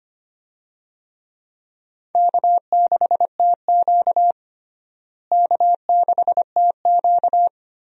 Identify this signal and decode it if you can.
Morse code, heard as a keyed tone, about 25 wpm, 700 Hz. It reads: K6TQ K6TQ
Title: USA Call Signs - 4 Characters - Repeated 3x - 25wpm